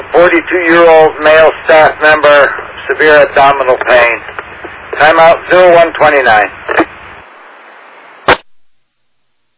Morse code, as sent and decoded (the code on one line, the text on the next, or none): none